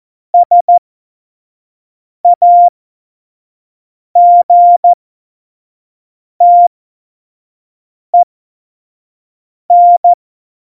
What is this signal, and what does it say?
Morse code, heard as a keyed tone, about 14 wpm, spaced as for 5 wpm, 700 Hz. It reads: SAGTEN